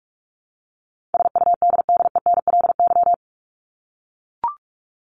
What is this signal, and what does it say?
Morse code, heard as a keyed tone, about 44 wpm, 700 Hz. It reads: SUDDENLY
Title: Top 1501-1750 Words 45wpm